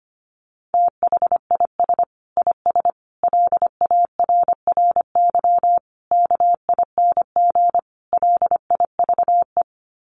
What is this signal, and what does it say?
Morse code, heard as a keyed tone, about 25 wpm, 700 Hz. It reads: THIS IS LARRY KING LIVE
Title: Sets of 5 Words Encore2 25wpm